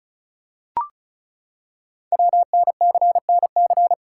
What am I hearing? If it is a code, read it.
Morse code, heard as a keyed tone, about 36 wpm, 700 Hz. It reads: WNCNC
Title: Code Groups - 5 Letters - 35wpm